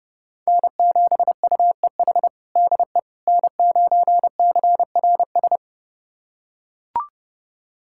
Morse code, heard as a keyed tone, about 32 wpm, 700 Hz. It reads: N7UEH DE N9CRS
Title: USA Callsign DE Callsign - 5 Characters - 30wpm